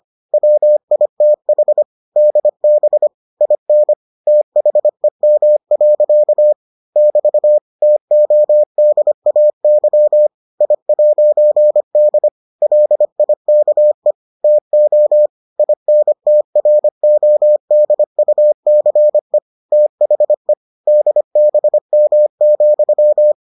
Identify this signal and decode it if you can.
Morse code, heard as a keyed tone, about 25 wpm, 600 Hz. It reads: WITH DB IN THEM. <BT> TODAY I'D LIKE TO INTRODUCE THE DBM,